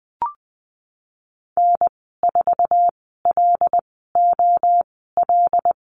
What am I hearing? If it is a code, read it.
Morse code, heard as a keyed tone, about 20 wpm, spaced as for 15 wpm, 700 Hz. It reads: N4LOL
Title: 193 Lesson 31g - Letter v - Call Signs 20wpm